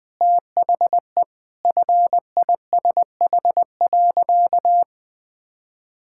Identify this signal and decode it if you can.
Morse code, heard as a keyed tone, about 21 wpm, 700 Hz. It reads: THE FISH.
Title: Sentences with 8-Word Phrases - 20wpm